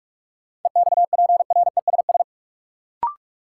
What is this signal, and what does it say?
Morse code, heard as a keyed tone, about 41 wpm, 700 Hz. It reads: EXPRESS
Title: Sentences from Top 1100 Words - New Word Drill - 45wpm